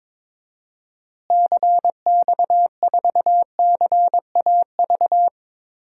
Morse code, heard as a keyed tone, about 22 wpm, 700 Hz. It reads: CX4CAV
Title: DXCC-100 Call Signs - 6 Characters - Repeated 3x - 22wpm